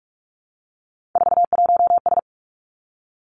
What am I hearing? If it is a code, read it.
Morse code, heard as a keyed tone, about 46 wpm, 700 Hz. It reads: 41S
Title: Three Challenging Letter-Number Combinations 45wpm